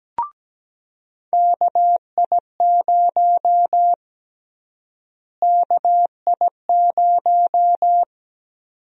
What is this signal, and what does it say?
Morse code, heard as a keyed tone, about 17 wpm, 700 Hz. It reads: KI0 KI0
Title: USA Call Sign Prefixes - Repeated 2x - 17wpm